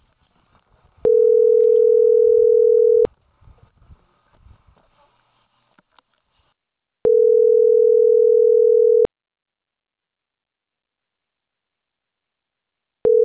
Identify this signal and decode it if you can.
Ringback tone (North America), 440 + 480 Hz.